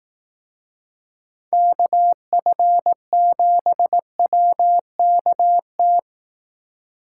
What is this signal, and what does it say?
Morse code, heard as a keyed tone, about 18 wpm, 700 Hz. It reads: KF7WKT
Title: Speed Racing - USA Call Signs 15wpm